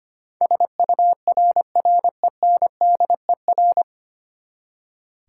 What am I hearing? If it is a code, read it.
Morse code, heard as a keyed tone, about 25 wpm, 700 Hz. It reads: SURRENDER
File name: Nine Letter Words - 25wpm